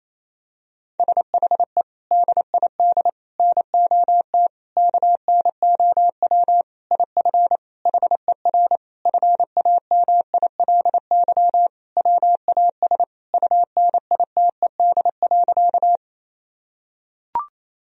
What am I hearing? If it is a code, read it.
Morse code, heard as a keyed tone, about 28 wpm, 700 Hz. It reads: SHE DID NOT KNOW IF HER FAMILY WAS UNITED.
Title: Sentences from Top 1000 Words - Review Part 4 - 28wpm